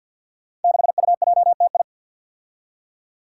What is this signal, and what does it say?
Morse code, heard as a keyed tone, about 48 wpm, 700 Hz. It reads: 6UJTI
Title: Five Challenging Letter-Number Combinations 50wpm